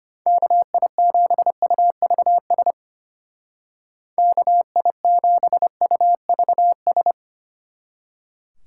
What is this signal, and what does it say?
Morse code, heard as a keyed tone, about 32 wpm, 700 Hz. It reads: KI7UVS KI7UVS